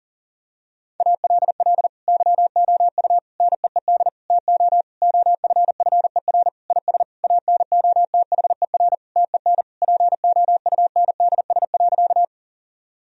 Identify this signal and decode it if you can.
Morse code, heard as a keyed tone, about 40 wpm, 700 Hz. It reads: ALL YOU NEED TO OFFER IS ANOTHER TEN POUNDS.
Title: Sentences from Top 1000 Words - Review Part 4 - 40wpm